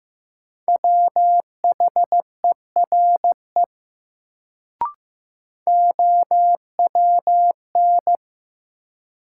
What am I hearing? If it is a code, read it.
Morse code, heard as a keyed tone, about 15 wpm, 700 Hz. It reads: WHERE OWN
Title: Sets of 2 Words - New Word Drill - 15wpm